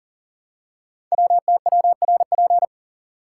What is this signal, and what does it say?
Morse code, heard as a keyed tone, about 43 wpm, 690 Hz. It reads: WTWRP
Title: Code Groups - 5 Letters - 40wpm